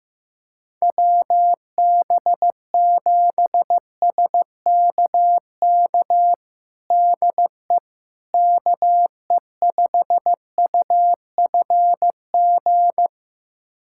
Morse code, heard as a keyed tone, about 15 wpm, 700 Hz. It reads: WB7SKK DE KE5UFG
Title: USA Callsign DE Callsign - 6 Characters - 15wpm